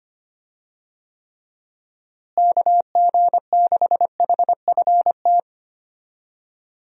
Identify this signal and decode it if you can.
Morse code, heard as a keyed tone, about 25 wpm, 700 Hz. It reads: KG6HFT
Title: USA Call Signs - 6 Characters - Repeated 3x - 25wpm